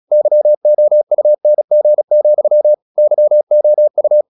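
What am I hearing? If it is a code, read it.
Morse code, heard as a keyed tone, about 36 wpm, 600 Hz. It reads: YOUNG, YOU
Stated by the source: Morse practice at 36 wpm, text of Ken Follett's novel Never, from CW med Gnister.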